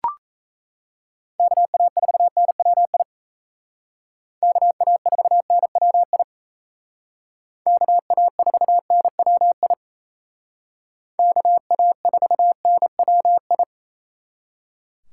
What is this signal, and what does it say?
Morse code, heard as a keyed tone, about 36 wpm, 700 Hz. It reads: KA4NWI KA4NWI KA4NWI KA4NWI